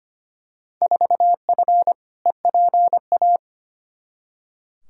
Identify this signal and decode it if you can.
Morse code, heard as a keyed tone, about 25 wpm, 700 Hz. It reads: 4F EPA
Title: Field Day Exchanges Encore - 25wpm